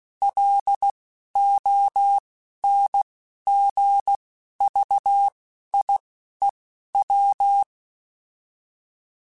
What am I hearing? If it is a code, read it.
Morse code, heard as a keyed tone, about 16 wpm, spaced as for 11 wpm, 800 Hz. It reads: LONGVIEW